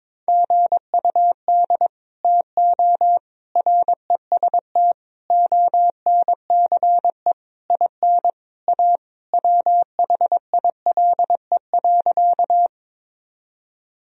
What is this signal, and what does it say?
Morse code, heard as a keyed tone, about 21 wpm, 700 Hz. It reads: GUD TO REST ONCE IN A WHILE.
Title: Sentences from Top 500 Words and Abbreviations 22wpm